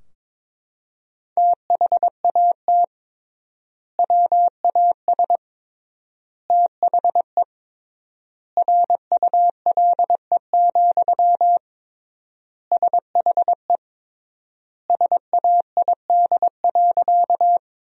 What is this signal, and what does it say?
Morse code, heard as a keyed tone, about 22 wpm, 700 Hz. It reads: THAT WAS THE RULE, SHE SAID.
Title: 3x Word-Spacing - Sentences from Top 400 Words - 22wpm